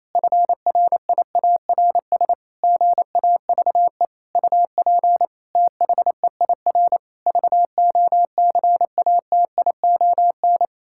Morse code, heard as a keyed tone, about 28 wpm, 700 Hz. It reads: FRIARS GAVE UP THEIR VOCATION
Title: Pangram Challenge v2 - 28wpm